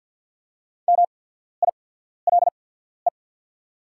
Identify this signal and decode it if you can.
Morse code, heard as a keyed tone, about 48 wpm, spaced as for 14 wpm, 690 Hz. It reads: MILE
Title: 8x Character Spacing - Top 500 Words - 50wpm